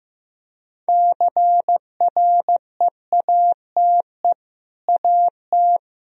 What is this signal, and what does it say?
Morse code, heard as a keyed tone, about 15 wpm, 700 Hz. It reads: CREATE AT
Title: Sentences from Top 1000 Words - Review Part 3 - 15wpm